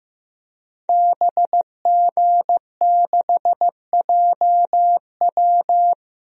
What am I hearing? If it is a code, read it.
Morse code, heard as a keyed tone, about 15 wpm, 700 Hz. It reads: BG6JW
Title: DXCC-100 Call Signs - 5 Characters - 15wpm